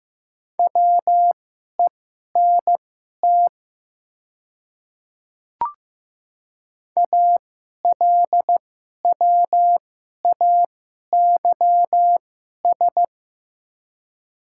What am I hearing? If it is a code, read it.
Morse code, heard as a keyed tone, about 15 wpm, spaced as for 11 wpm, 700 Hz. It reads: WENT ALWAYS
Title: 2x Character Spacing - Top 500 Words - 15wpm